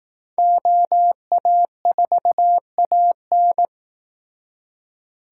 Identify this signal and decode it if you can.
Morse code, heard as a keyed tone, about 18 wpm, 700 Hz. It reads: OA4AN
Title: Speed Racing - USA 2x2 Call Signs 15wpm